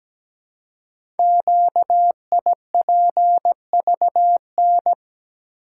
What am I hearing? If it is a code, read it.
Morse code, heard as a keyed tone, about 17 wpm, 700 Hz. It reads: QIPVN
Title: Code Groups - 5 Letter:Numbers - 17wpm